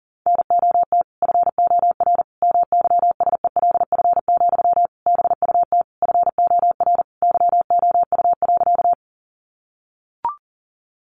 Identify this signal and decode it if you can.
Morse code, heard as a keyed tone, about 40 wpm, 700 Hz. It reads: NOT FOR MYSELF, BUT FOR YOU.
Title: Sentences from Top 1300 Words 40wpm